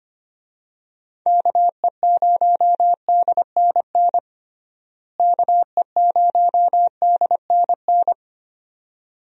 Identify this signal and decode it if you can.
Morse code, heard as a keyed tone, about 25 wpm, 700 Hz. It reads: KE0DNN KE0DNN
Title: USA Call Signs - 6 Characters - Repeated 2x - 25wpm